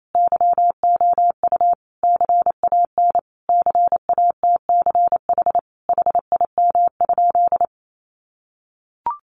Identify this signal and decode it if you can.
Morse code, heard as a keyed tone, about 28 wpm, 700 Hz. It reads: YOU CAN CATCH HIM?
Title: Sentences from Top 1500 Words 28wpm